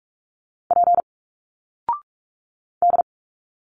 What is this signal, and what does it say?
Morse code, heard as a keyed tone, about 45 wpm, 700 Hz. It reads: P D
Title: Single Letters 45wpm